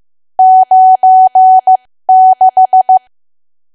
Morse code, heard as a keyed tone, about 15 wpm, spaced as for 13 wpm, 750 Hz. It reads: 96
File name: cwa / W1AW-Level2 / 13WPM